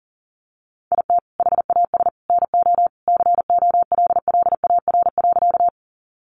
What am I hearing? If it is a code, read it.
Morse code, heard as a keyed tone, about 40 wpm, 700 Hz. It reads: IT HAS NO COLLAR.
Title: Easy Phrases 40wpm